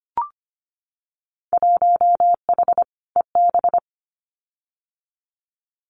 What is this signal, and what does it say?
Morse code, heard as a keyed tone, about 25 wpm, 700 Hz. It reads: 1H EB
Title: Winter Field Day Exchanges Encore 25wpm